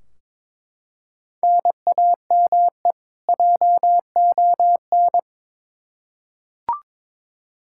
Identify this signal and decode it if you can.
Morse code, heard as a keyed tone, about 22 wpm, 700 Hz. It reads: NAME JON
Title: K1USN SST QSO Practice 22wpm